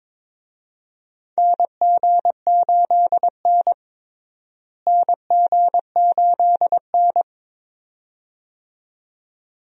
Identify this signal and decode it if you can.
Morse code, heard as a keyed tone, about 22 wpm, 700 Hz. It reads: NG8N NG8N
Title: USA Call Signs - 4 Characters - Repeated 2x - 22wpm